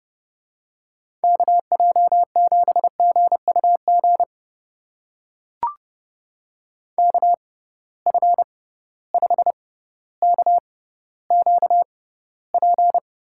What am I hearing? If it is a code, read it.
Morse code, heard as a keyed tone, about 30 wpm, spaced as for 15 wpm, 700 Hz. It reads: KJ7GUG KF5KQP
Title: USA Call Signs - 6 Characters - 6x Character Spacing - 30wpm